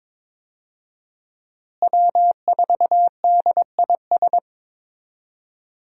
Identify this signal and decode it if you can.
Morse code, heard as a keyed tone, about 22 wpm, 700 Hz. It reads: W4DIS
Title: USA Call Signs - 5 Characters - Repeated 3x - 22wpm